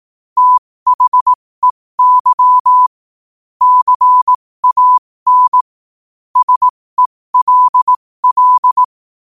Morse code, heard as a keyed tone, about 18 wpm, spaced as for 15 wpm, 1000 Hz. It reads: THEY CAN SELL
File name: qotdfiles / 15wpm